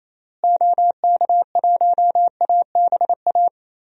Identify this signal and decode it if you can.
Morse code, heard as a keyed tone, about 28 wpm, 700 Hz. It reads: OK1ABA
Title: DXCC-100 Call Signs - 6 Characters - Repeated 3x - 28wpm